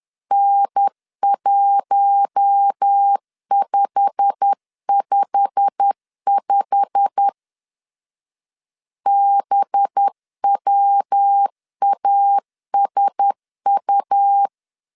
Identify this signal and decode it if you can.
Morse code, heard as a keyed tone, about 10 wpm, 800 Hz. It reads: N1555 BWASU
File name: LEZIONI / 8 YJ45 / 8 YJ45 10wpm 2549